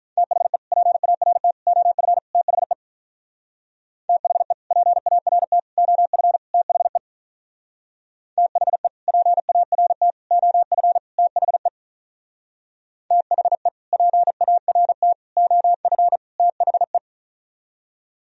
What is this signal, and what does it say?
Morse code, heard as a keyed tone, about 46 wpm, 700 Hz. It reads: THE PART OF THE THE PART OF THE THE PART OF THE THE PART OF THE